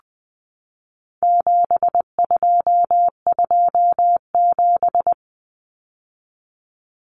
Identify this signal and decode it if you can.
Morse code, heard as a keyed tone, about 20 wpm, 700 Hz. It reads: 7227
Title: Common Cut Numbers - 4-Digit 20wpm